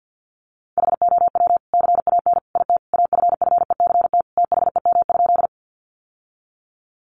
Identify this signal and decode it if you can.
Morse code, heard as a keyed tone, about 50 wpm, 700 Hz. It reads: HOW CAN IT AFFECT THEM?
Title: Sentences from Top 900 Words 50wpm